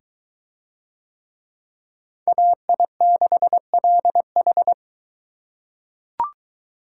Morse code, heard as a keyed tone, about 22 wpm, 700 Hz. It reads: AI6LH